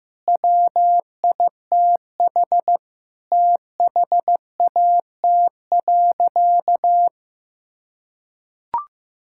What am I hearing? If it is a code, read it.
Morse code, heard as a keyed tone, about 15 wpm, 700 Hz. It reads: WITH THAT.
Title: Sentences from Top 1000 Words - Review Part 5 - 15wpm